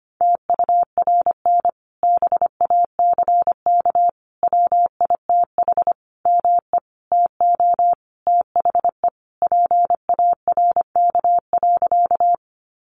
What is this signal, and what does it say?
Morse code, heard as a keyed tone, about 25 wpm, 700 Hz. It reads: TURN BACK WITH ME TO THE PARK.
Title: Sentences from Top 1300 Words 25wpm